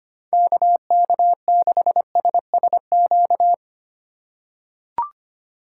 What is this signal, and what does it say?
Morse code, heard as a keyed tone, about 25 wpm, 700 Hz. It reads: KK6SSQ
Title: USA Call Signs - 4x Character Spacing - 25wpm